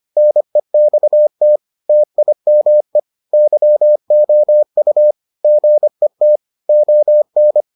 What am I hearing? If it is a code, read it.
Morse code, heard as a keyed tone, about 25 wpm, 600 Hz. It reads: NEXT TIME YOU GET ON